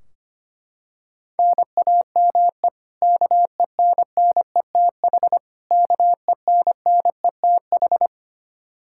Morse code, heard as a keyed tone, about 25 wpm, 700 Hz. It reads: NAME KENNETH KENNETH
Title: Practice QSO - Exchange 1 - Beginning - 25wpm